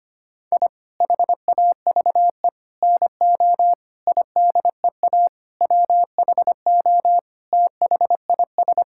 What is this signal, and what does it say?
Morse code, heard as a keyed tone, about 25 wpm, 700 Hz. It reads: I HAVE NO IDEA WHO THIS
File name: practice / 25wpm / Sentences with 9-Word Phrases - 25wpm